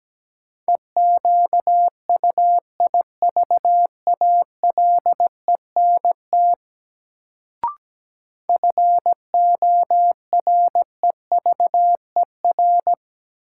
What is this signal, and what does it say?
Morse code, heard as a keyed tone, about 17 wpm, 700 Hz. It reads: EQUIVALENT FOREVER
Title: Sentences from Top 1600 Words - New Word Drill Alt - 17wpm